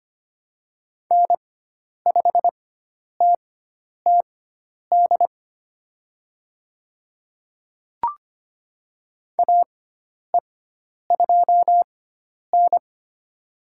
Morse code, heard as a keyed tone, about 25 wpm, spaced as for 10 wpm, 700 Hz. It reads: N5TTD AE2N
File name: Lesson 16g - Letter c - Call Signs 25wpm